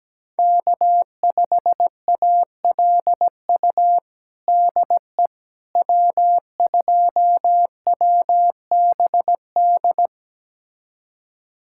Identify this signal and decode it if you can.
Morse code, heard as a keyed tone, about 17 wpm, 700 Hz. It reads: K5ALU DE W2WBD
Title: USA Callsign DE Callsign - 5 Characters - 17wpm